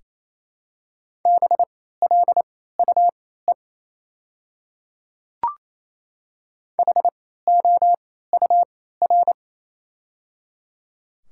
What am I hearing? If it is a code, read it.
Morse code, heard as a keyed tone, about 28 wpm, spaced as for 16 wpm, 690 Hz. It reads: BLUE HOUR
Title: 3x Character Spacing - Top 500 Words - 28wpm